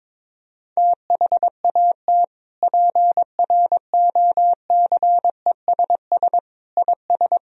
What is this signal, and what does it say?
Morse code, heard as a keyed tone, about 22 wpm, 700 Hz. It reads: THAT PROCESS IS
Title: Sentences from Top 1000 Words - Review Part 3 - 22wpm